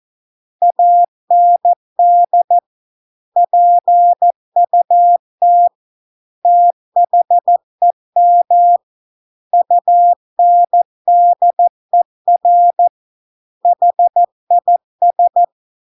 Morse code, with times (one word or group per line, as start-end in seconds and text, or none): 0.6-2.6 AND
3.4-5.7 PUT
6.5-8.8 THEM
9.5-12.9 UNDER
13.7-15.4 HIS